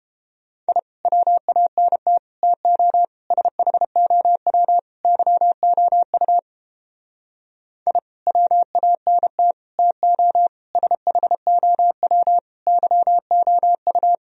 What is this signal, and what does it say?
Morse code, heard as a keyed tone, about 31 wpm, 700 Hz. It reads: I WANT TO SHOW YOU I WANT TO SHOW YOU